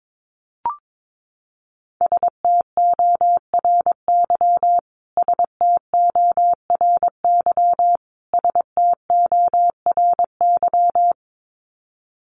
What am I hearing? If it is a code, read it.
Morse code, heard as a keyed tone, about 22 wpm, 700 Hz. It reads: STORY STORY STORY
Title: Top 300 Words Repeated 22wpm